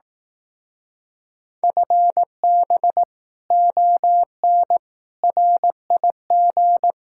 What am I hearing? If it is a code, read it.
Morse code, heard as a keyed tone, about 18 wpm, 700 Hz. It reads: FB ON RIG
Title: Speed-Racing Common QSO Phrases Encore - 15wpm